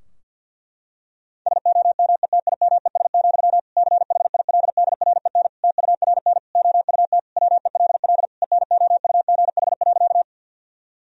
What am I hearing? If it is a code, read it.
Morse code, heard as a keyed tone, about 52 wpm, 700 Hz. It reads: SOMETIMES, CHILDREN TURN OUT WELL ENOUGH.
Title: Sentences from Top 1000 Words - Review Part 3 - 50wpm